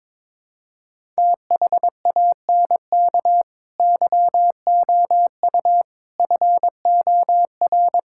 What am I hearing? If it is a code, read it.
Morse code, heard as a keyed tone, about 22 wpm, 700 Hz. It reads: THANK YOU FOR
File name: practice / 22wpm / Sets of 7 Words - 22wpm